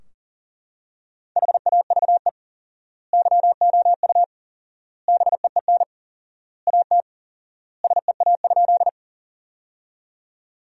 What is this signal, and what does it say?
Morse code, heard as a keyed tone, about 40 wpm, 700 Hz. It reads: HAVE YOU BEEN AT SEA?
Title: 4x Word-Spacing - Sentences from Top 300 Words - 40wpm